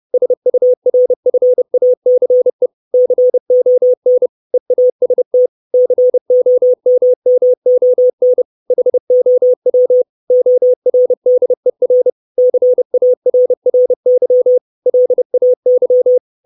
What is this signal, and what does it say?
Morse code, heard as a keyed tone, about 30 wpm, 500 Hz. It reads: SURFACE CON EAST COMMON HOW ORDER CARRY LAY
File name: 632 Words 30WPM